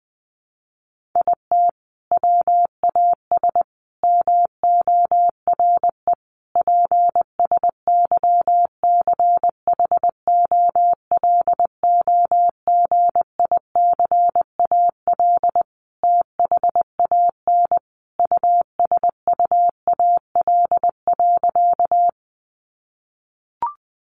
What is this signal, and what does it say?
Morse code, heard as a keyed tone, about 19 wpm, 700 Hz. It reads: IT WAS MORE PSYCHOLOGICAL THAN USUAL.
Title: Sentences from Top 1400 Words 20wpm